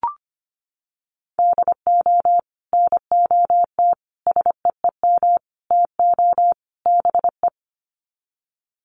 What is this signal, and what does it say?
Morse code, heard as a keyed tone, about 25 wpm, 700 Hz. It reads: DO NOT SEEM TO BE
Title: Sets of 5 Words Encore2 25wpm